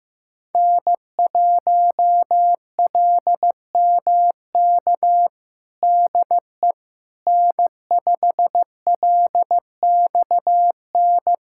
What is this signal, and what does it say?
Morse code, heard as a keyed tone, about 15 wpm, 700 Hz. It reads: N1LMK DE N5LXN
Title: USA Callsign DE Callsign - 5 Characters - 15wpm